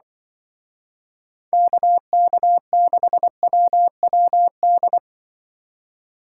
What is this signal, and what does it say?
Morse code, heard as a keyed tone, about 24 wpm, 700 Hz. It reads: KK6WWD
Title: Speed Racing - USA 2x3 Call Signs 20wpm